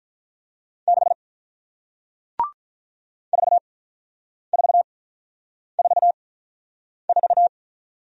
Morse code, heard as a keyed tone, about 48 wpm, 700 Hz. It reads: 6 4 4 4 4